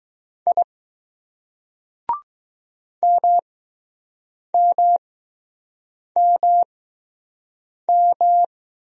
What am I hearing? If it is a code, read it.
Morse code, heard as a keyed tone, about 20 wpm, spaced as for 6 wpm, 700 Hz. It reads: I MMMM